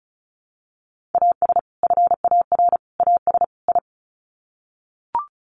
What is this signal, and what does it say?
Morse code, heard as a keyed tone, about 34 wpm, 700 Hz. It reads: AS FAR AS I
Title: Sets of 4 Words 35wpm